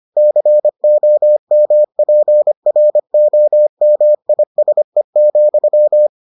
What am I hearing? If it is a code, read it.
Morse code, heard as a keyed tone, about 25 wpm, 600 Hz. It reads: COMPROMISE,